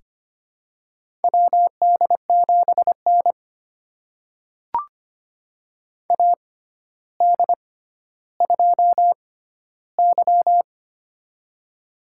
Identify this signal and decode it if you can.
Morse code, heard as a keyed tone, about 25 wpm, spaced as for 13 wpm, 700 Hz. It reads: WD7N AD2Y
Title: USA Call Signs - 4 Characters - 6x Character Spacing - 25wpm